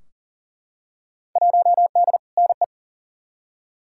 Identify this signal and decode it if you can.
Morse code, heard as a keyed tone, about 40 wpm, 700 Hz. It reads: 1D NE
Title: Field Day Exchanges - 40wpm